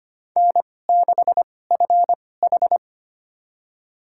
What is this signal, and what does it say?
Morse code, heard as a keyed tone, about 25 wpm, spaced as for 18 wpm, 700 Hz. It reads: N6FH
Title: USA Call Signs - 4 Characters - 2x Character Spacing - 25wpm